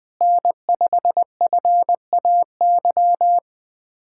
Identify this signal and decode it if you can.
Morse code, heard as a keyed tone, about 19 wpm, 700 Hz. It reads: N5FAY
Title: USA Callsign DE Callsign - 5 Characters - 20wpm